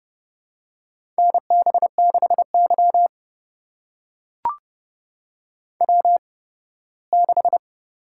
Morse code, heard as a keyed tone, about 30 wpm, spaced as for 18 wpm, 700 Hz. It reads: NB6Y W6
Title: USA Call Signs - 8x Character Spacing - 30wpm